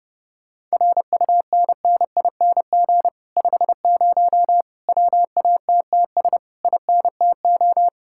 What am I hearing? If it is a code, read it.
Morse code, heard as a keyed tone, about 33 wpm, 700 Hz. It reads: RUNNING 50 WATTS INTO